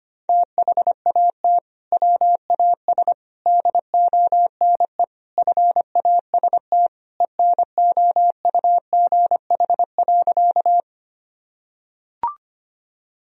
Morse code, heard as a keyed tone, about 25 wpm, 700 Hz. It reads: THAT WAS DONE FAST ENOUGH.
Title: Sentences from Top 1000 Words - Review Part 2 - 25wpm